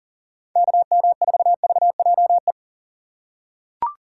Morse code, heard as a keyed tone, about 40 wpm, 700 Hz. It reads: KM4VJE
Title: USA Call Signs - 4x Character Spacing - 40wpm